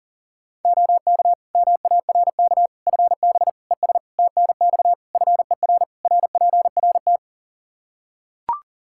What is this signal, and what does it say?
Morse code, heard as a keyed tone, about 40 wpm, 700 Hz. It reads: OK MARK FB ES TNX FER RPRT
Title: Practice QSO - Exchange 2 - Info - 40wpm